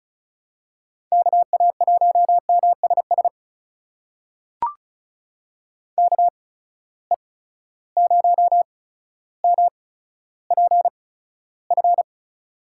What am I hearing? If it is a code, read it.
Morse code, heard as a keyed tone, about 34 wpm, spaced as for 15 wpm, 700 Hz. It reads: KA1MSS KE0MPF
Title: USA Call Signs - 6 Characters - 8x Character Spacing - 35wpm